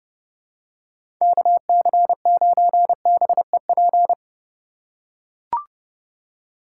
Morse code, heard as a keyed tone, about 26 wpm, 700 Hz. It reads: KC9BEP